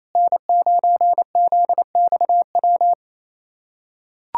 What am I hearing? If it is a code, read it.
Morse code, heard as a keyed tone, about 28 wpm, 700 Hz. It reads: N9ZXW
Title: SOTA QSO Practice 28wpm